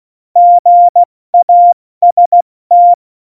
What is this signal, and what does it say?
Morse code, heard as a keyed tone, about 16 wpm, 700 Hz. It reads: GAST